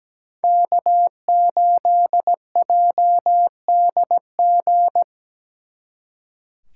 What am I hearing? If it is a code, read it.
Morse code, heard as a keyed tone, about 20 wpm, 700 Hz. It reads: K8JDG
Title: Speed Racing - USA 1x3 Call Signs 17wpm